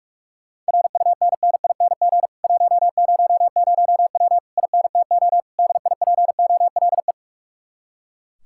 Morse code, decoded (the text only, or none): RUNNING 100W INTO DIPOLE